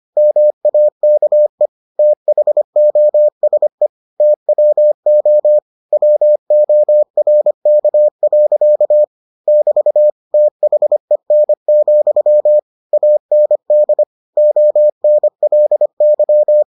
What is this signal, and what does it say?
Morse code, heard as a keyed tone, about 25 wpm, 600 Hz. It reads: MAKE THOSE TWO WORK. <BT> THEN, AND ONLY